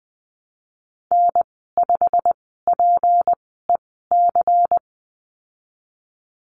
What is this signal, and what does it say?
Morse code, heard as a keyed tone, about 20 wpm, spaced as for 15 wpm, 700 Hz. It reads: N5PEC
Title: 161 Lesson 26g - Letter g - Call Signs 20wpm